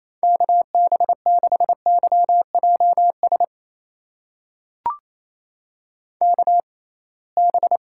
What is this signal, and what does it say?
Morse code, heard as a keyed tone, about 28 wpm, spaced as for 21 wpm, 700 Hz. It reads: KB6YJS KB
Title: USA Call Signs - 6 Characters - 6x Character Spacing - 28wpm